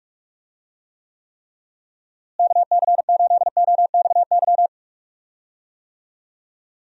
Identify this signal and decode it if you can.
Morse code, heard as a keyed tone, about 43 wpm, 700 Hz. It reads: KC8OXY
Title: USA Call Signs - 6 Characters - Repeated 3x - 45wpm